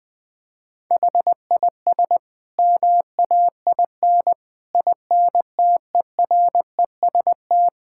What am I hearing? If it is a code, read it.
Morse code, heard as a keyed tone, about 21 wpm, 700 Hz. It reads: HIS MAIN INTEREST
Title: Sentences from Top 1000 Words - Review Part 6 - 20wpm